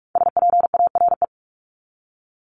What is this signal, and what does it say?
Morse code, heard as a keyed tone, about 44 wpm, 700 Hz. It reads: SPARE
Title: Top 1751-2000 Words 45wpm